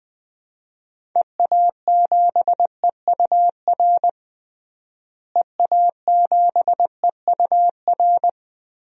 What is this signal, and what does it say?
Morse code, heard as a keyed tone, about 20 wpm, 700 Hz. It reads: EA7EUR EA7EUR
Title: DXCC-100 Call Signs - 6 Characters - Repeated 2x - 20wpm